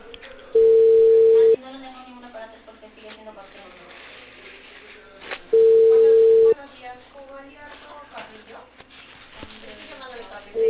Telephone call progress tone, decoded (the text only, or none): ringback tone